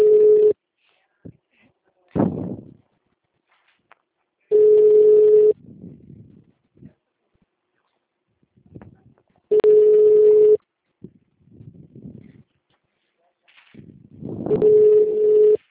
Ringback tone (Europe, 425 Hz).
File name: out- -2008- - - .14677